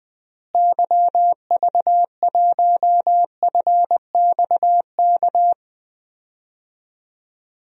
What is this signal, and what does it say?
Morse code, heard as a keyed tone, about 20 wpm, 700 Hz. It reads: YV1FXK
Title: DXCC-100 Call Signs - 6 Characters - Repeated 3x - 20wpm